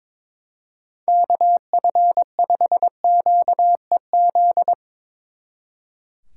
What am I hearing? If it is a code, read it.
Morse code, heard as a keyed tone, about 22 wpm, 700 Hz. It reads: KF5QEZ
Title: Speed Racing - USA 2x3 Call Signs 22wpm